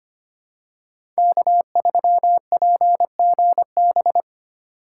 Morse code, heard as a keyed tone, about 25 wpm, 700 Hz. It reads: K3PGB
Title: USA Call Signs - 5 Characters - 2x Character Spacing - 25wpm